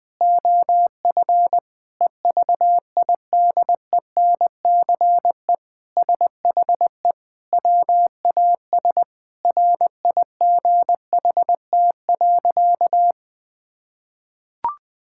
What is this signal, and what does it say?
Morse code, heard as a keyed tone, about 20 wpm, 700 Hz. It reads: OF EVIDENCE SHE WAS RIGHT.
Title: Sentences with 4-Word Phrases - Spoken Beginning - 20wpm